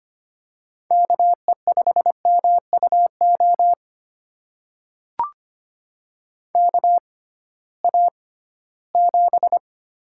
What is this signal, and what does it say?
Morse code, heard as a keyed tone, about 25 wpm, spaced as for 16 wpm, 700 Hz. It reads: KE5MUO KA7
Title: USA Call Signs - 6 Characters - 6x Character Spacing - 25wpm